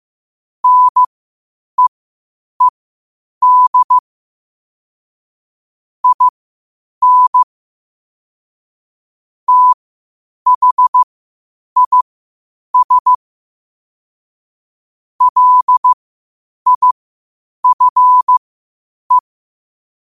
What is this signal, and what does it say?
Morse code, heard as a keyed tone, about 15 wpm, spaced as for 8 wpm, 1000 Hz. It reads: NEED IN THIS LIFE
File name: qotdfiles / 8wpm